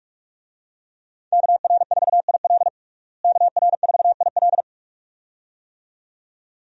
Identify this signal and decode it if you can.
Morse code, heard as a keyed tone, about 41 wpm, 700 Hz. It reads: KR4IL KR4IL
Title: USA Call Signs - 5 Characters - Repeated 2x - 45wpm